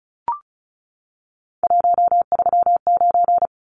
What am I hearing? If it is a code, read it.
Morse code, heard as a keyed tone, about 34 wpm, 700 Hz. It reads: 139
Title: Numbers - 3-Digit 35wpm